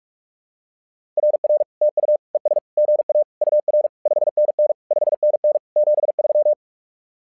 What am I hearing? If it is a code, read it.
Morse code, heard as a keyed tone, about 43 wpm, 600 Hz. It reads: RR TU ES GA UR 5NN 5NN 73
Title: SOTA QSO Practice 45wpm